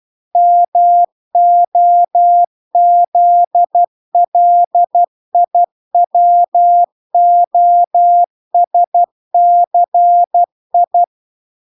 Morse code, heard as a keyed tone, about 12 wpm, 700 Hz. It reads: MOZLIWOSCI